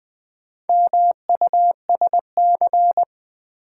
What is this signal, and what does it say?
Morse code, heard as a keyed tone, about 20 wpm, 700 Hz. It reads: MUSC
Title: Build A Word - 2 Letters At A Time - 6-Letter Words - 20wpm